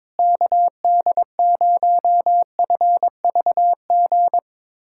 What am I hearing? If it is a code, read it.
Morse code, heard as a keyed tone, about 22 wpm, 700 Hz. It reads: KD0FVG
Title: USA Callsign DE Callsign - 6 Characters - 22wpm